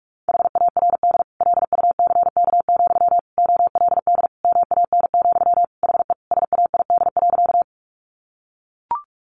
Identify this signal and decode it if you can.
Morse code, heard as a keyed tone, about 44 wpm, 700 Hz. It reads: HARD LUCK, OLD MAN, HE SAID.
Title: Sentences from Top 1500 Words 45wpm